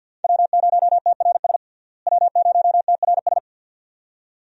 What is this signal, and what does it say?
Morse code, heard as a keyed tone, about 49 wpm, 700 Hz. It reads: W0TRS W0TRS
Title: USA Call Signs - 5 Characters - Repeated 2x - 50wpm